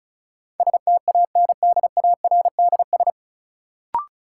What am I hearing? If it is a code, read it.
Morse code, heard as a keyed tone, about 34 wpm, 700 Hz. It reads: STANDARDS
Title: Sentences from Top 1000 Words - Review Part 6 - New Word Drill - 35wpm